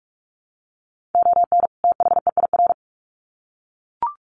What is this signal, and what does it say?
Morse code, heard as a keyed tone, about 44 wpm, 700 Hz. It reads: ON THEIR
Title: Sets of 2 Words 45wpm